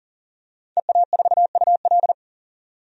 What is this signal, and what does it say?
Morse code, heard as a keyed tone, about 40 wpm, 700 Hz. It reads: EA4UL
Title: DXCC-100 Call Signs - 5 Characters - 40wpm